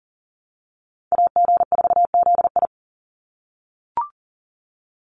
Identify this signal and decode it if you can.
Morse code, heard as a keyed tone, about 40 wpm, 700 Hz. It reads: AG4ZI
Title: USA Call Signs 40wpm